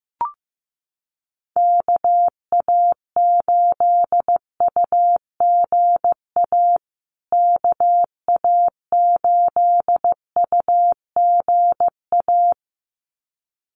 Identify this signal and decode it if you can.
Morse code, heard as a keyed tone, about 15 wpm, 700 Hz. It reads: KA8UGA KA8UGA
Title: USA Call Signs Repeated 2x 15wpm